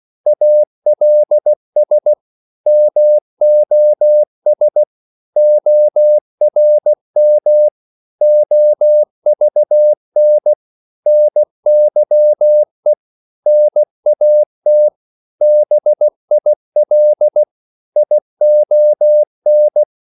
Morse code, heard as a keyed tone, about 16 wpm, 600 Hz. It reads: ALS MOS ORM OVN NYE NAT BIL ION
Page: Korte danske ord 16wpm